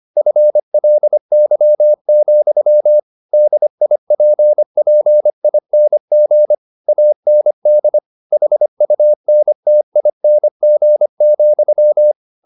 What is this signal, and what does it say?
Morse code, heard as a keyed tone, about 25 wpm, 600 Hz. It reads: FLY, DIPPING AND HUNTING,
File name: the old man and the sea part 1 25wpm